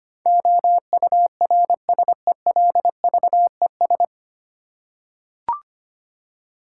Morse code, heard as a keyed tone, about 25 wpm, 700 Hz. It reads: OURSELVES
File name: Nine Letter Words - 25wpm